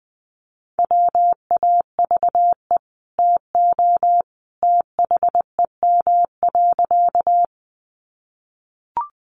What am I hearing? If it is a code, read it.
Morse code, heard as a keyed tone, about 20 wpm, 700 Hz. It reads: WAVE TO THEM.
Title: Sentences from Top 1400 Words 20wpm